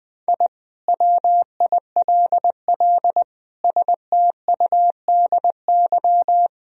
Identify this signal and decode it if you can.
Morse code, heard as a keyed tone, about 20 wpm, 700 Hz. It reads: I WILL STUDY
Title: Sentences from Top 1000 Words - Review Part 2 - 20wpm